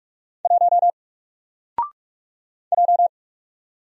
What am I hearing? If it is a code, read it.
Morse code, heard as a keyed tone, about 44 wpm, 700 Hz. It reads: 1 J
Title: Single Character (Letter:Number:Punctuation:Prosign) Alt 45wpm